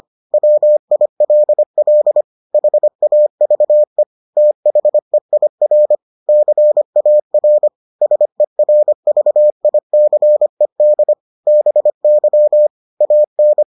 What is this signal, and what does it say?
Morse code, heard as a keyed tone, about 25 wpm, 600 Hz. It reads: WILL HAVE THEIR CAR SERVICED BY AN